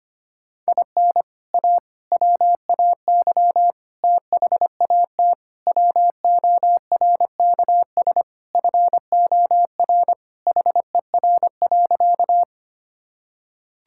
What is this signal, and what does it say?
Morse code, heard as a keyed tone, about 25 wpm, 700 Hz. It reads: IN A WAY THAT WORKS FOR HER.
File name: practice / 25wpm / Sentences with 4-Word Phrases - 25wpm